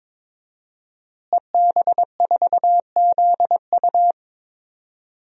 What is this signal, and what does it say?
Morse code, heard as a keyed tone, about 21 wpm, 700 Hz. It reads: EB4ZU
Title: DXCC-100 Call Signs - 5 Characters - Repeated 2x - 22wpm